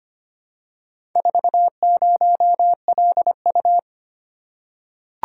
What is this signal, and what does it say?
Morse code, heard as a keyed tone, about 25 wpm, 700 Hz. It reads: 40LU
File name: practice / 25wpm / Four Letter-Number Combinations - 25wpm